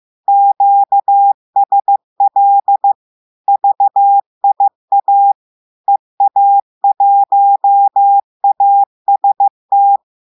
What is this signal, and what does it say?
Morse code, heard as a keyed tone, about 15 wpm, 800 Hz. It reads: QSL VIA EA1AST
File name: eh1ard-amateur-radio-day-2016 mf feed 1